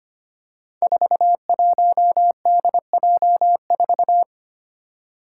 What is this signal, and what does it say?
Morse code, heard as a keyed tone, about 25 wpm, 700 Hz. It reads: 41DJ4
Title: Five Challenging Letter-Number Combinations 25wpm